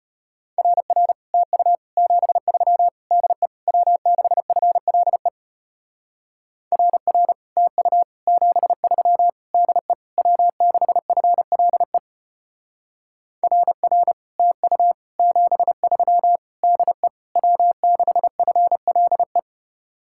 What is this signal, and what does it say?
Morse code, heard as a keyed tone, about 32 wpm, 700 Hz. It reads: RR TU 73 DE W6FLE RR TU 73 DE W6FLE RR TU 73 DE W6FLE